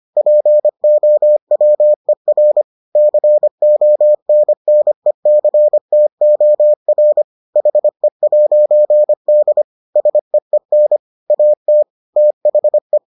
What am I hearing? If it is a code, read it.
Morse code, heard as a keyed tone, about 25 wpm, 600 Hz. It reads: POWER CONNECTOR HE'D SEEN AT THE